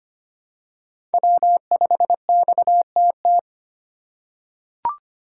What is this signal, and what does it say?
Morse code, heard as a keyed tone, about 25 wpm, 700 Hz. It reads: W5XTT
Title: USA Call Signs - 5 Characters - 4x Character Spacing - 25wpm